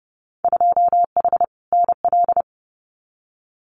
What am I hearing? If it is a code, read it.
Morse code, heard as a keyed tone, about 30 wpm, 700 Hz. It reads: 2H NL
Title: Winter Field Day Exchanges Encore 30wpm